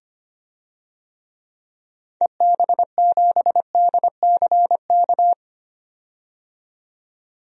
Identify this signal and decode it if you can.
Morse code, heard as a keyed tone, about 25 wpm, 700 Hz. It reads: EB7DCK
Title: DXCC-100 Call Signs - 6 Characters - 25wpm